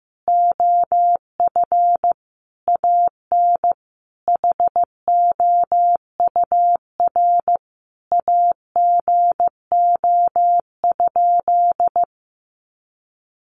Morse code, heard as a keyed tone, about 15 wpm, 700 Hz. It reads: OF AN HOUR AGO?
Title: Sentences from Top 1300 Words 15wpm